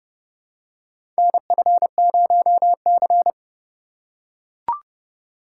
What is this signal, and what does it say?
Morse code, heard as a keyed tone, about 30 wpm, 700 Hz. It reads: NF0C
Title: USA Call Signs - 4 Characters - 4x Character Spacing - 30wpm